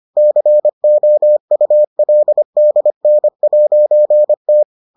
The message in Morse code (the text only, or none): COULDN'T